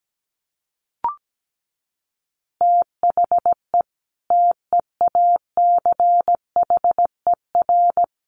Morse code, heard as a keyed tone, about 17 wpm, 700 Hz. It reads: THE TEACHER
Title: Additional Easy Phrases 17wpm